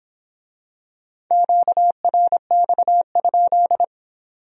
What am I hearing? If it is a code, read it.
Morse code, heard as a keyed tone, about 23 wpm, 700 Hz. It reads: QRX?